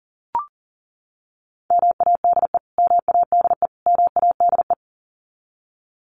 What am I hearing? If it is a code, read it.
Morse code, heard as a keyed tone, about 40 wpm, 700 Hz. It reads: MADE MADE MADE
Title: Top 300 Words Repeated 40wpm